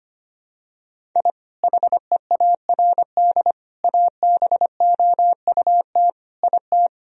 Morse code, heard as a keyed tone, about 25 wpm, 700 Hz. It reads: I HEARD ABOUT IT
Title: Sentences from Top 1000 Words - Review Part 1 - 25wpm